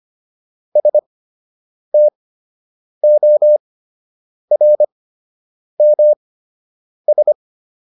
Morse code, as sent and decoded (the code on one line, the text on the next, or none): ... - --- .-. -- ...
STORMS